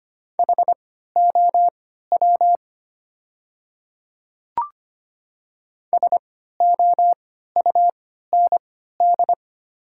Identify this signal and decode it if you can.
Morse code, heard as a keyed tone, about 25 wpm, spaced as for 14 wpm, 700 Hz. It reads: HOW SOUND
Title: 3x Character Spacing - Top 100 Words - 25wpm